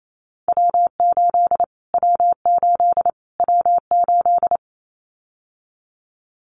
Morse code, heard as a keyed tone, about 28 wpm, 700 Hz. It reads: W8 W8 W8
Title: Callsign Prefixes for Top 25 DXCC - Repeated 3x - 28wpm